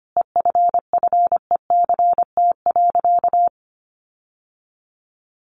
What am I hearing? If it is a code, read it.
Morse code, heard as a keyed tone, about 25 wpm, 700 Hz. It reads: EFFECT.
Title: Sentences from Top 1400 Words 25wpm